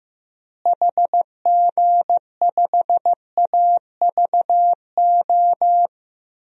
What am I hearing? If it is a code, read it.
Morse code, heard as a keyed tone, about 15 wpm, 700 Hz. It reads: HG5AVO
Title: DXCC-100 Call Signs - 6 Characters - Repeated 2x - 15wpm